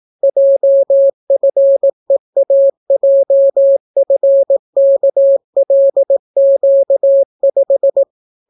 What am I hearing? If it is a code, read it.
Morse code, heard as a keyed tone, about 18 wpm, 550 Hz. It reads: JFEAJFKLQ5